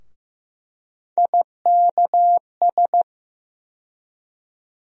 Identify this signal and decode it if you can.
Morse code, heard as a keyed tone, about 15 wpm, 700 Hz. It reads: IKS
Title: USA Call Sign Postfixes - 3 Letters 15wpm